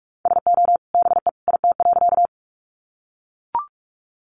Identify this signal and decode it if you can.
Morse code, heard as a keyed tone, about 44 wpm, 700 Hz. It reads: SO BE IT.
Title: Sentences from Top 100 Words 45wpm